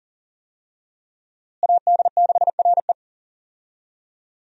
Morse code, heard as a keyed tone, about 38 wpm, 700 Hz. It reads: AD6RE
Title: USA Call Signs - 5 Characters - Repeated 3x - 40wpm